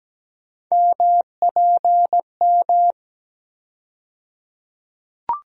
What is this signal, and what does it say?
Morse code, heard as a keyed tone, about 17 wpm, 700 Hz. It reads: MPM
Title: USA Call Sign Postfixes - 3 Letters 17wpm